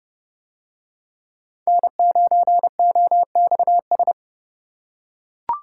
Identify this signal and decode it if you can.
Morse code, heard as a keyed tone, about 31 wpm, 700 Hz. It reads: N9OXS